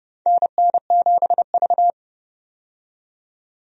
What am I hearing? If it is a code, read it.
Morse code, heard as a keyed tone, about 30 wpm, 700 Hz. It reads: NN7V
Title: USA Call Signs - 4 Characters - 30wpm